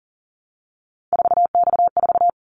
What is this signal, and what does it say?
Morse code, heard as a keyed tone, about 40 wpm, 700 Hz. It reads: 4X4
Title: Callsign Prefixes for Top 50 DXCC - 40wpm